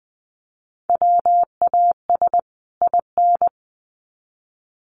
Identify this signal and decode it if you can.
Morse code, heard as a keyed tone, about 20 wpm, 700 Hz. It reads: WAS IN
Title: Sets of 2 Words 20wpm